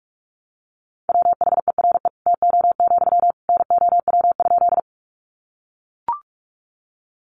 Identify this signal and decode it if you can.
Morse code, heard as a keyed tone, about 44 wpm, 700 Hz. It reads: WHERE TO, NOW?
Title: Sentences from Top 200 Words 45wpm